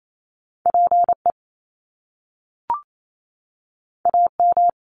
Morse code, heard as a keyed tone, about 28 wpm, 700 Hz. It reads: PE AM
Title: Top Two Letter Combinations in Words 28wpm